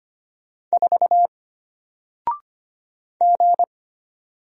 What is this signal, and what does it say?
Morse code, heard as a keyed tone, about 25 wpm, 700 Hz. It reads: 4 G
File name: Single-Letter-Number-25wpm